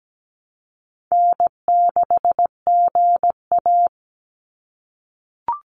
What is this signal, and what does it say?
Morse code, heard as a keyed tone, about 17 wpm, 700 Hz. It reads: N6GA
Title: Easy Call Signs - 17wpm